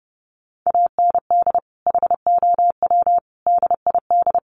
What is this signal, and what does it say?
Morse code, heard as a keyed tone, about 30 wpm, 700 Hz. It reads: AND HOW DID
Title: Sentences from Top 200 Words 30wpm